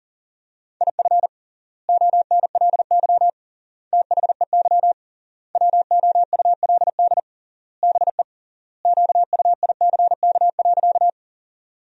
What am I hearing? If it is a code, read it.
Morse code, heard as a keyed tone, about 40 wpm, 700 Hz. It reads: IF ONLY THEY WOULD BE QUICK.
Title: 3x Word-Spacing - Sentences from Top 500 Words - 40wpm